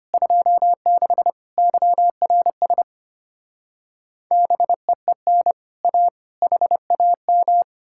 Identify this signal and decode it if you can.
Morse code, heard as a keyed tone, about 33 wpm, 700 Hz. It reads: 26 YRS BEEN A HAM